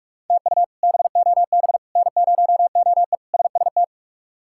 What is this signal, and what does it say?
Morse code, heard as a keyed tone, about 46 wpm, 700 Hz. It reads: TU BOB N0OE SST
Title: K1USN SST QSO Practice - 45wpm